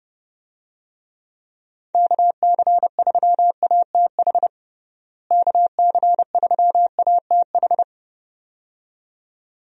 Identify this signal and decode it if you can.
Morse code, heard as a keyed tone, about 29 wpm, 700 Hz. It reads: KC3ATH KC3ATH
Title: USA Call Signs - 6 Characters - Repeated 2x - 30wpm